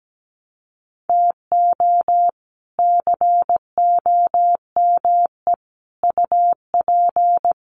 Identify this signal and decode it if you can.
Morse code, heard as a keyed tone, about 17 wpm, 700 Hz. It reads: TO COME UP
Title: Sets of 4 Words 17wpm